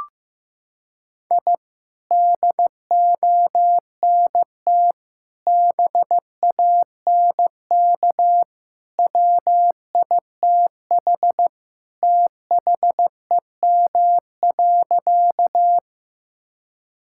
Morse code, heard as a keyed tone, about 15 wpm, 700 Hz. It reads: I DONT BANK WITH THEM.
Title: Sentences from Top 1000 Words - Review Part 6 - 15wpm